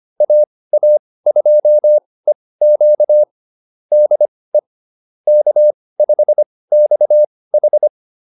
Morse code, heard as a keyed tone, about 25 wpm, spaced as for 18 wpm, 600 Hz. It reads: AA2EQ DE K5XH